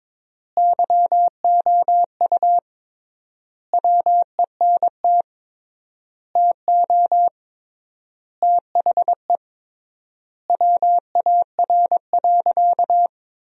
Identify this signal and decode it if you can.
Morse code, heard as a keyed tone, about 22 wpm, 700 Hz. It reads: YOU WENT TO THE WAR.
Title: 3x Word-Spacing - Sentences from Top 400 Words - 22wpm